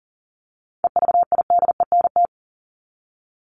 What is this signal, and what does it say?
Morse code, heard as a keyed tone, about 40 wpm, 700 Hz. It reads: EVIDENT
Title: LY Suffix - Top 100 Words with Base Words 40wpm